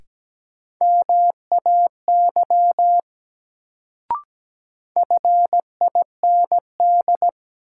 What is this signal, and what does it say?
Morse code, heard as a keyed tone, about 17 wpm, 700 Hz. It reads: MAY FIND
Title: Sentences from Top 100 Words - New Word Drill - 17wpm